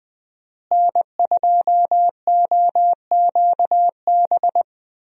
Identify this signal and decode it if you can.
Morse code, heard as a keyed tone, about 20 wpm, 700 Hz. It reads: N2OQB
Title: USA Call Signs - 5 Characters - Repeated 3x - 20wpm